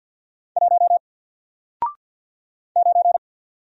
Morse code, heard as a keyed tone, about 48 wpm, 700 Hz. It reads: 1 9
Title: Single Letter-Number Alt 50wpm